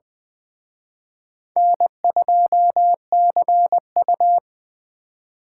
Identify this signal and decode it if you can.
Morse code, heard as a keyed tone, about 21 wpm, 700 Hz. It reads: N2CU
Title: Speed Racing - USA 1x2 Call Signs 17wpm